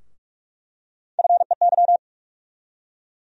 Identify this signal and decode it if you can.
Morse code, heard as a keyed tone, about 41 wpm, 700 Hz. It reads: FEY
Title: USA Call Sign Postfixes - 3 Letters 45wpm